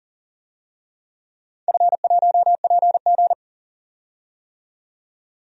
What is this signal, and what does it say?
Morse code, heard as a keyed tone, about 40 wpm, 700 Hz. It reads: F1PG
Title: DXCC-100 Call Signs - 4 Characters - 40wpm